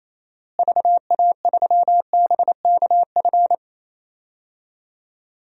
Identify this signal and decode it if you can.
Morse code, heard as a keyed tone, about 28 wpm, 700 Hz. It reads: VA3BKF
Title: DXCC-100 Call Signs - 6 Characters - Repeated 3x - 28wpm